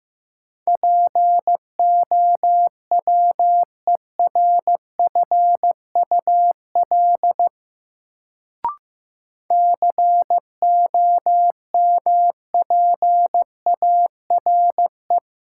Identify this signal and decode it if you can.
Morse code, heard as a keyed tone, about 15 wpm, 700 Hz. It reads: POWERFUL COMPARE
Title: Sentences from Top 1300 Words - New Word Drill Alt - 15wpm